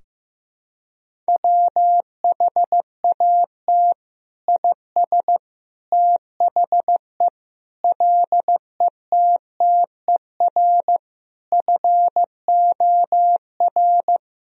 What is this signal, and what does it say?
Morse code, heard as a keyed tone, about 16 wpm, 700 Hz. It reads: WHAT IS THE LETTER FOR
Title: Sentences from Top 1000 Words - Review Part 2 - 15wpm